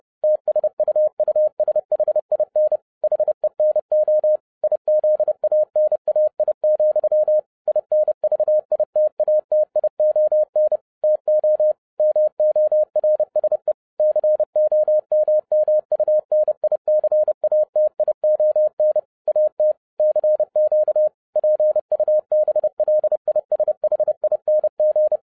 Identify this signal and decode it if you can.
Morse code, heard as a keyed tone, about 30 wpm, 610 Hz. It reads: TSUUSHIN HENO IZANAI, INVITATION TO MORSE COMMUNICATION AT CQ PUBLISHING